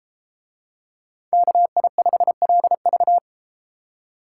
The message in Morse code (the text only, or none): KI5LV